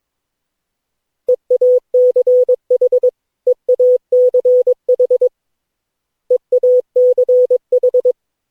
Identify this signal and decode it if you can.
Morse code, heard as a keyed tone, about 22 wpm, 500 Hz. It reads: EACH EACH EACH
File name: cw / Session9 cw training